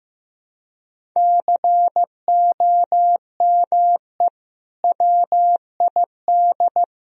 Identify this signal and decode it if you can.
Morse code, heard as a keyed tone, about 15 wpm, 700 Hz. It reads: COME WID
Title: Sentences from Top 200 Words and Abbreviations 15wpm